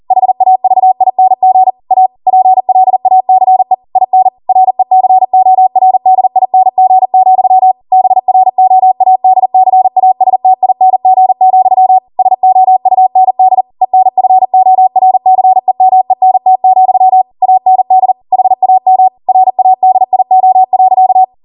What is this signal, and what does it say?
Morse code, heard as a keyed tone, about 40 wpm, 750 Hz. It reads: HAVING A PLACE IN RECORDING, BROADCASTING, SOUND ENFORCEMENT, AND HAM RADIO.